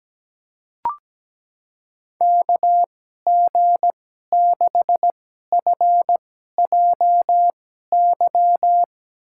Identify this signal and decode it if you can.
Morse code, heard as a keyed tone, about 17 wpm, spaced as for 12 wpm, 700 Hz. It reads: KG6FJY
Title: USA Call Signs - 6 Characters - 2x Character Spacing - 17wpm